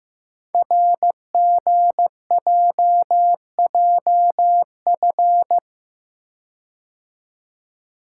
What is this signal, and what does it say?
Morse code, heard as a keyed tone, about 15 wpm, 700 Hz. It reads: RGJJF
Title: Code Groups - 5 Letters - 15wpm